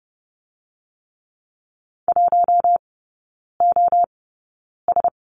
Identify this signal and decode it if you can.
Morse code, heard as a keyed tone, about 30 wpm, spaced as for 9 wpm, 700 Hz. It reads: 1OS
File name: 081 Lesson 14e - Number 2 - Three Characters 30wpm